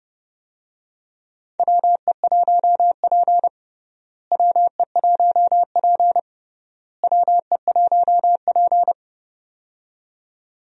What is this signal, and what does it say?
Morse code, heard as a keyed tone, about 31 wpm, 700 Hz. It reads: WE1P WE1P WE1P
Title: USA Call Signs - 4 Characters - Repeated 3x - 30wpm